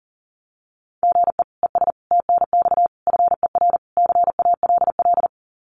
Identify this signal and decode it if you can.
Morse code, heard as a keyed tone, about 40 wpm, 700 Hz. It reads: GE ES TNX FER CALL
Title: Common QSO Phrases Encore 40wpm